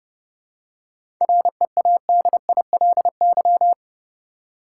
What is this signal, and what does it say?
Morse code, heard as a keyed tone, about 31 wpm, 700 Hz. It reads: READILY
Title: Seven Letter Words - 30wpm